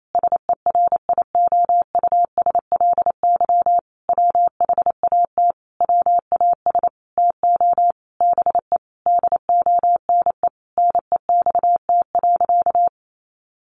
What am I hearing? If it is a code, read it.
Morse code, heard as a keyed tone, about 28 wpm, 700 Hz. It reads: SERIOUSLY WHAT WAS TO BE DONE NEXT.
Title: Sentences from Top 1200 Words 28wpm